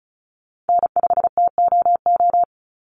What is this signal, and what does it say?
Morse code, heard as a keyed tone, about 34 wpm, 700 Hz. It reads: N5TOO
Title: Call Signs - 35wpm